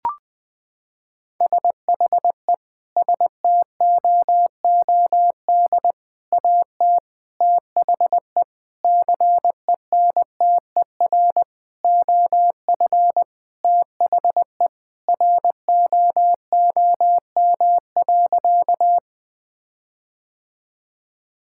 Morse code, heard as a keyed tone, about 21 wpm, 700 Hz. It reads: SHE STOOD AT THE CENTER OF THE ROOM.
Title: Sentences with 5-Word Phrases - 20wpm